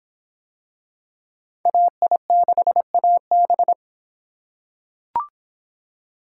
Morse code, heard as a keyed tone, about 23 wpm, 700 Hz. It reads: AI6AB